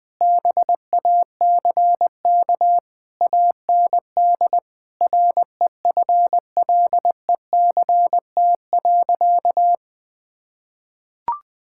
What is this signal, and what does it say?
Morse code, heard as a keyed tone, about 20 wpm, 700 Hz. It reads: BACK AND REFLECT.
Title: Sentences with 6-Word Phrases - 20wpm